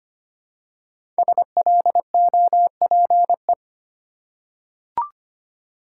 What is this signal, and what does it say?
Morse code, heard as a keyed tone, about 25 wpm, 700 Hz. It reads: SLOPE
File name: 5 Letter Words - First 3 Letters Given - 25wpm